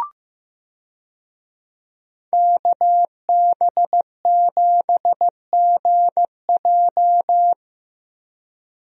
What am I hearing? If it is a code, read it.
Morse code, heard as a keyed tone, about 15 wpm, 700 Hz. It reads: KB7GJ
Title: USA Call Signs - 5 Characters - Repeated 3x - 15wpm